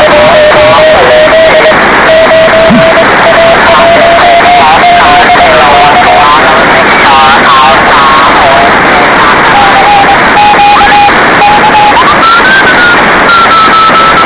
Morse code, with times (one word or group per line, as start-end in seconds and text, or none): none